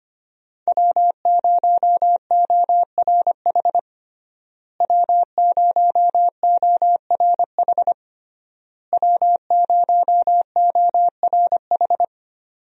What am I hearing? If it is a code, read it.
Morse code, heard as a keyed tone, about 25 wpm, 700 Hz. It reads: W0ORH W0ORH W0ORH
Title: USA Call Signs - 5 Characters - Repeated 3x - 25wpm